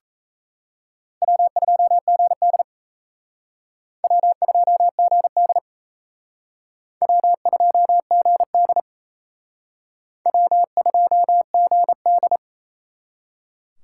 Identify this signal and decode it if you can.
Morse code, heard as a keyed tone, about 35 wpm, 700 Hz. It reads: W2GD W2GD W2GD W2GD
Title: Speed Racing - USA Call Signs 28wpm